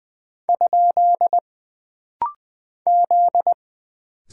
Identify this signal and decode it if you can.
Morse code, heard as a keyed tone, about 20 wpm, 700 Hz. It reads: ? Z
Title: Single Character (Letter:Number:Punctuation:Prosign) - Rapid-Fire - 20wpm